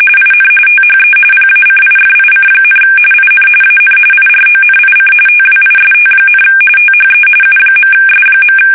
RTTY, about 45.45 baud. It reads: IN THE PACIFIC, IN AN IRONIC TWIST, THE U.S.
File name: fsk850